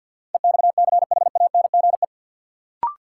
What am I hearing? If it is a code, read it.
Morse code, heard as a keyed tone, about 53 wpm, 700 Hz. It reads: EXCHANGE
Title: Sentences from Top 1000 Words - Review Part 6 - New Word Drill - 50wpm